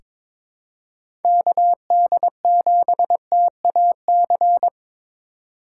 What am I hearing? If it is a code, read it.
Morse code, heard as a keyed tone, about 22 wpm, 700 Hz. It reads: KD7TAC
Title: USA Call Signs - 6 Characters - 2x Character Spacing - 22wpm